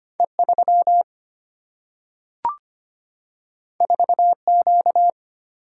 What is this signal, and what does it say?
Morse code, heard as a keyed tone, about 25 wpm, 700 Hz. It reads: E3 4Q
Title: Single-Letter Single-Number 25wpm